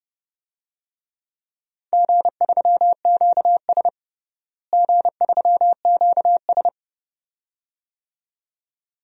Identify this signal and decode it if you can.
Morse code, heard as a keyed tone, about 31 wpm, 700 Hz. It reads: G3QS G3QS
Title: DXCC-100 Call Signs - 4 Characters - Repeated 2x - 30wpm